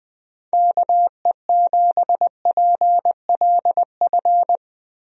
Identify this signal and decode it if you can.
Morse code, heard as a keyed tone, about 20 wpm, 700 Hz. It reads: KE7PLF
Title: USA Call Signs - 6 Characters - Repeated 3x - 20wpm